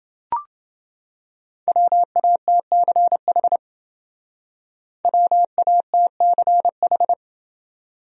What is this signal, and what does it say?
Morse code, heard as a keyed tone, about 29 wpm, 700 Hz. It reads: WATCH WATCH